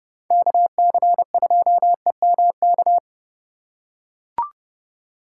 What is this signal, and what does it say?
Morse code, heard as a keyed tone, about 32 wpm, 700 Hz. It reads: KC2EMK
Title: USA Call Signs - 3x Character Spacing - 30wpm